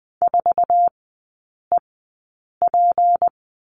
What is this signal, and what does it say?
Morse code, heard as a keyed tone, about 20 wpm, spaced as for 8 wpm, 690 Hz. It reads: 4EP
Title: 159 Lesson 26e - Letter g - Three Characters 20wpm